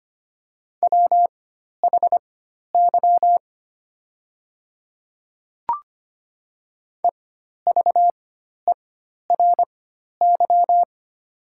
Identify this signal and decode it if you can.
Morse code, heard as a keyed tone, about 25 wpm, spaced as for 12 wpm, 700 Hz. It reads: WHY EVERY
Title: 4x Character Spacing - Top 200 Words - 25wpm